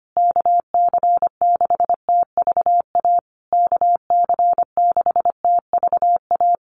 Morse code, heard as a keyed tone, about 25 wpm, 700 Hz. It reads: KC6TVA KC6TVA
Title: USA Call Signs Repeated 3x 25wpm